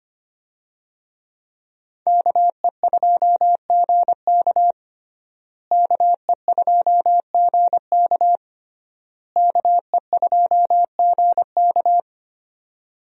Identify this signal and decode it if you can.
Morse code, heard as a keyed tone, about 25 wpm, 700 Hz. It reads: KE2GK KE2GK KE2GK
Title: USA Call Signs - 5 Characters - Repeated 3x - 25wpm